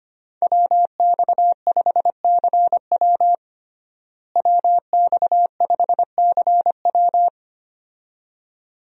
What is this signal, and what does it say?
Morse code, heard as a keyed tone, about 25 wpm, 700 Hz. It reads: WX5CW WX5CW
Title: USA Call Signs - 5 Characters - Repeated 2x - 25wpm